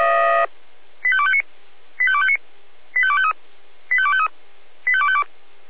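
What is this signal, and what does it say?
Five-tone selective call (ZVEI), heard as quick runs of five tones, tones of about 70 ms. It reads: 85259 85259 85253 85253 85252
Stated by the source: ZVEI five-tone call received by SDR.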